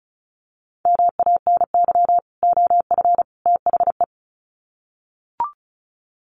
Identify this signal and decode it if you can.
Morse code, heard as a keyed tone, about 34 wpm, 700 Hz. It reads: MANY OF THE
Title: Sets of 3 Words 35wpm